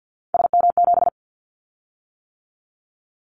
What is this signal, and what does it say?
Morse code, heard as a keyed tone, about 50 wpm, 700 Hz. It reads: SM7
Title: Callsign Prefixes for Top 25 DXCC - 50wpm